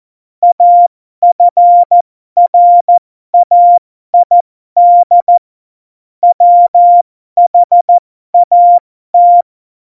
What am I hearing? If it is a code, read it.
Morse code, heard as a keyed tone, about 14 wpm, spaced as for 12 wpm, 700 Hz. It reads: AFRAID WHAT
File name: Morsekurs / Witze